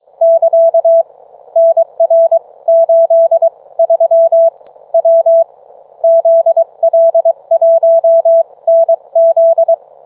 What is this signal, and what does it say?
Morse code, heard as a keyed tone, about 23 wpm, spaced as for 19 wpm, 660 Hz. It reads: <KA> NR83 W ZL1NZ